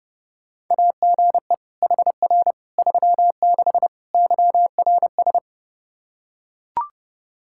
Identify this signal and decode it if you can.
Morse code, heard as a keyed tone, about 32 wpm, 700 Hz. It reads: AGE HR 36 YRS
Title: Practice QSO - Exchange 3 - More Info - 30wpm